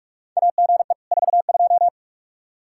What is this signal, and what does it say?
Morse code, heard as a keyed tone, about 46 wpm, 700 Hz. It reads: AGE 42
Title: Practice QSO - Exchange 3 - More Info - 45wpm